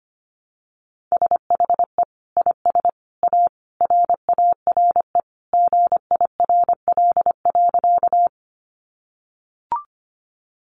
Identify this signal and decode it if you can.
Morse code, heard as a keyed tone, about 25 wpm, 700 Hz. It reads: SHE IS A RARE GIRL.
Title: Sentences from Top 1100 Words 25wpm